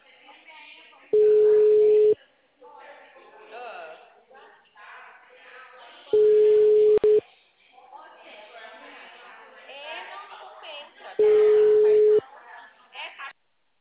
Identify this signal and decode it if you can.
Ringback tone (Europe, 425 Hz), with speech.